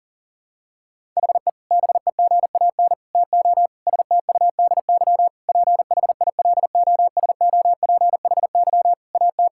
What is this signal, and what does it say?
Morse code, heard as a keyed tone, about 40 wpm, 700 Hz. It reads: HE BEGAN TO STUDY PHILOSOPHY AT
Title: Sentences from Top 1000 Words - Review Part 6 - 40wpm